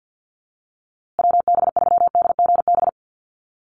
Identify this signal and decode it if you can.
Morse code, heard as a keyed tone, about 50 wpm, 700 Hz. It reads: WB3DGB
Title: USA Call Signs Repeated 3x 50wpm